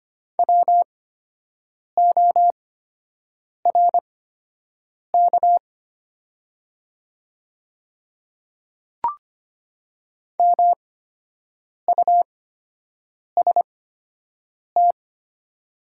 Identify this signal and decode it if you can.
Morse code, heard as a keyed tone, about 25 wpm, spaced as for 7 wpm, 700 Hz. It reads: WORK MUST
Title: 8x Character Spacing - Top 200 Words - 25wpm